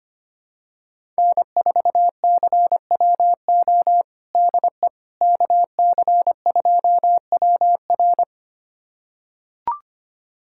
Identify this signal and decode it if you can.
Morse code, heard as a keyed tone, about 25 wpm, 700 Hz. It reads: N4CWO DE KC2WR
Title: USA Callsign DE Callsign - 5 Characters - 25wpm